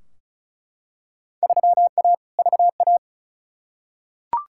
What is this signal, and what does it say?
Morse code, heard as a keyed tone, about 36 wpm, 690 Hz. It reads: 3A VA
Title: Field Day Exchanges Encore - 35wpm